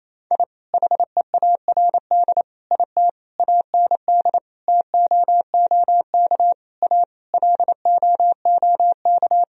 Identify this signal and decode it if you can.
Morse code, heard as a keyed tone, about 28 wpm, 700 Hz. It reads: I HEARD IT AND TOOK A LOOK
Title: Sentences from Top 1000 Words - Review Part 1 - 28wpm